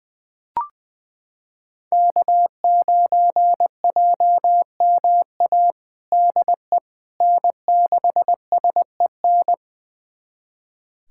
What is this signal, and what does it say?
Morse code, heard as a keyed tone, about 20 wpm, 700 Hz. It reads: K9JMA DE N6SEN
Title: USA Callsign DE Callsign - 5 Characters - 20wpm